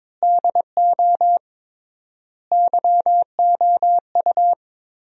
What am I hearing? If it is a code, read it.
Morse code, heard as a keyed tone, about 22 wpm, 700 Hz. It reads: DO YOU
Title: 3x Word-Spacing - Sentences from Top 200 Words - 22wpm